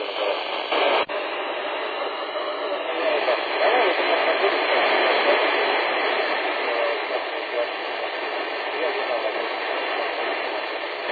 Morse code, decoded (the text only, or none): none